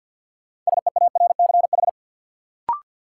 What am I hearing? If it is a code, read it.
Morse code, heard as a keyed tone, about 51 wpm, 700 Hz. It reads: SEARCH
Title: Sentences from Top 800 Words - New Word Drill - 50wpm